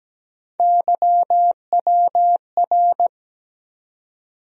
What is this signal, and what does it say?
Morse code, heard as a keyed tone, about 17 wpm, 700 Hz. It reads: YWR
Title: USA Call Sign Postfixes - 3 Letters 17wpm